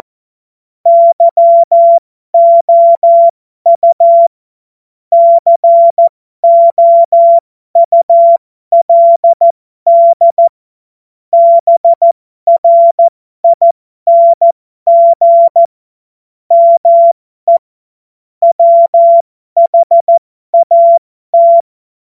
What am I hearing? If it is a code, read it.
Morse code, heard as a keyed tone, about 14 wpm, spaced as for 12 wpm, 680 Hz. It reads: YOU COULD BRING ME WHAT
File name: Morsekurs / Witze